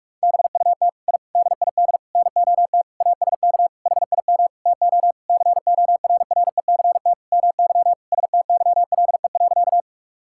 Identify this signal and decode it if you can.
Morse code, heard as a keyed tone, about 46 wpm, 700 Hz. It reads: BUT I DID NOT ASK HIM TO CORRECT MY STYLE.
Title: Sentences from Top 1000 Words - Review Part 5 - 45wpm